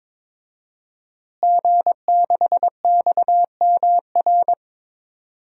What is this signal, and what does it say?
Morse code, heard as a keyed tone, about 22 wpm, 700 Hz. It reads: G6XMR
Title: DXCC-100 Call Signs - 5 Characters - Repeated 2x - 22wpm